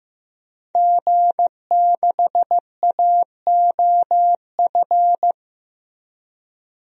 Morse code, heard as a keyed tone, about 15 wpm, 700 Hz. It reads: G6AOF
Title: DXCC-100 Call Signs - 5 Characters - 15wpm